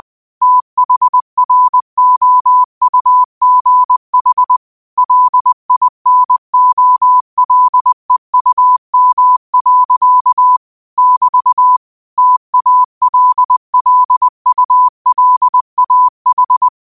Morse code, heard as a keyed tone, about 20 wpm, 1000 Hz. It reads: THROUGH LINOLEUM. <BT> TALLULAH